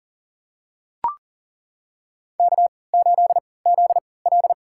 Morse code, heard as a keyed tone, about 40 wpm, spaced as for 23 wpm, 700 Hz. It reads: K8ZL
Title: USA Call Signs - 4 Characters - 3x Character Spacing - 40wpm